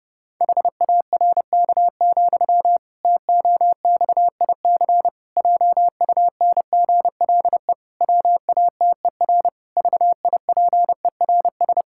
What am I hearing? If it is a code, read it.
Morse code, heard as a keyed tone, about 30 wpm, 700 Hz. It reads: HARK, TOXIC JUNGLE WATER VIPERS
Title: Pangram Challenge v2 - 30wpm